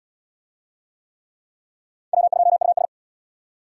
Morse code, heard as a keyed tone, about 65 wpm, 700 Hz. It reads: W2US